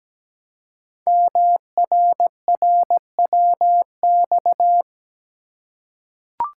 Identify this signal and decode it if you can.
Morse code, heard as a keyed tone, about 17 wpm, 700 Hz. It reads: MRRWX
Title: Code Groups - 5 Letters - 17wpm